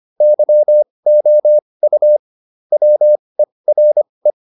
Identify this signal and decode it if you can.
Morse code, heard as a keyed tone, about 25 wpm, spaced as for 20 wpm, 600 Hz. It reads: YOU WERE